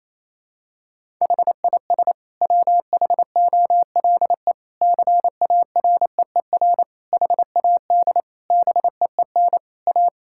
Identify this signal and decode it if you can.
Morse code, heard as a keyed tone, about 28 wpm, 700 Hz. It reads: HIS WHOLE CAREER HAD BEEN A
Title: Sentences from Top 1000 Words - Review Part 6 - 28wpm